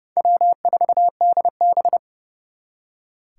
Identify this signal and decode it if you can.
Morse code, heard as a keyed tone, about 38 wpm, 700 Hz. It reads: W4DB